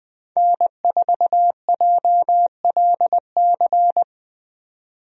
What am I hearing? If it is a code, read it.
Morse code, heard as a keyed tone, about 20 wpm, 700 Hz. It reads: N4JLC
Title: USA Call Signs - 5 Characters - 20wpm